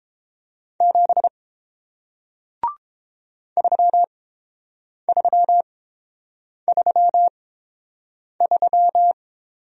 Morse code, heard as a keyed tone, about 31 wpm, spaced as for 7 wpm, 700 Hz. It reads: 7 3333